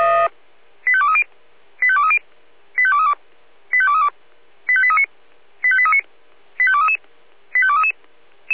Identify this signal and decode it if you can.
Five-tone selective call (ZVEI), heard as quick runs of five tones, tones of about 70 ms. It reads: 85230 85230 85232 85232 85839 85839 85233 85233